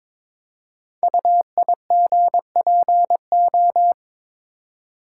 Morse code, heard as a keyed tone, about 22 wpm, 700 Hz. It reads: UIGPO
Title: Code Groups - 5 Letters - 22wpm